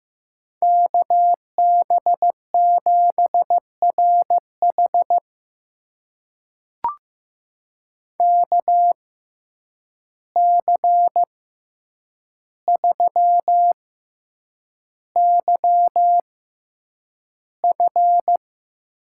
Morse code, heard as a keyed tone, about 15 wpm, spaced as for 8 wpm, 700 Hz. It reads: KB7RH KC3YF
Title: USA Call Signs - 5 Characters - 6x Character Spacing - 15wpm